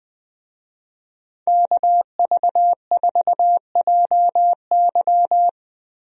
Morse code, heard as a keyed tone, about 20 wpm, 700 Hz. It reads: KV4JY
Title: USA Call Signs - 5 Characters - Repeated 3x - 20wpm